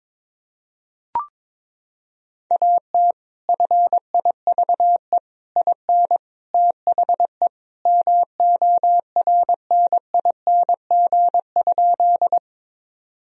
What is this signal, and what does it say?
Morse code, heard as a keyed tone, about 22 wpm, 700 Hz. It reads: AT FIVE IN THE MORNING?
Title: Sentences from Top 1000 Words - Review Part 2 - 22wpm